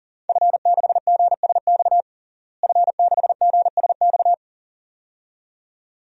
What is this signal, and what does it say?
Morse code, heard as a keyed tone, about 40 wpm, 700 Hz. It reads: F6GSX F6GSX
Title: DXCC-100 Call Signs - 5 Characters - Repeated 2x - 40wpm